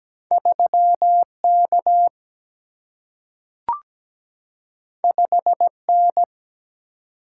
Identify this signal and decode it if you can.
Morse code, heard as a keyed tone, about 17 wpm, 700 Hz. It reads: 3 K 5 N
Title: Single-Letter Single-Number 17wpm